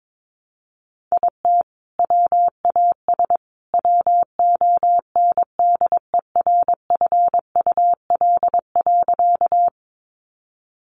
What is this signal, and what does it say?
Morse code, heard as a keyed tone, about 22 wpm, 700 Hz. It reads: IT WAS WONDERFUL.
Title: Sentences from Top 1100 Words 22wpm